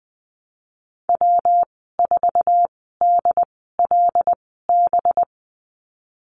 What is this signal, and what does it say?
Morse code, heard as a keyed tone, about 19 wpm, spaced as for 14 wpm, 700 Hz. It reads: W4DLB
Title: 193 Lesson 31g - Letter v - Call Signs 20wpm